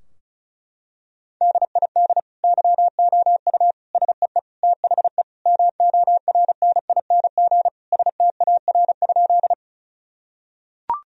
Morse code, heard as a keyed tone, about 36 wpm, 700 Hz. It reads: DID YOU SEE THE MORNING STAR?
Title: Sentences from Top 1000 Words - Review Part 2 - 35wpm